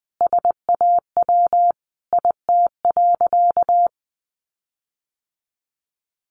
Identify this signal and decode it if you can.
Morse code, heard as a keyed tone, about 20 wpm, 700 Hz. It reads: SAW IT.
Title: Sentences from Top 1500 Words 20wpm